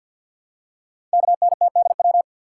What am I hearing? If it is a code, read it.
Morse code, heard as a keyed tone, about 50 wpm, 690 Hz. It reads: KNTDW
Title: Code Groups - 5 Letters - 50wpm